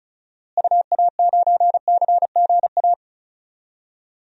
Morse code, heard as a keyed tone, about 36 wpm, 700 Hz. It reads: UA9CGA
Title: DXCC-100 Call Signs - 6 Characters - Repeated 3x - 35wpm